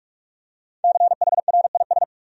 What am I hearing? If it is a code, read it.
Morse code, heard as a keyed tone, about 46 wpm, 700 Hz. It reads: CHRIS
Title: USA Common Names - 45wpm